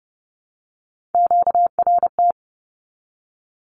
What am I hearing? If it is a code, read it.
Morse code, heard as a keyed tone, about 30 wpm, 700 Hz. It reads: QRT
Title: Q-Codes Repeated 3x 30wpm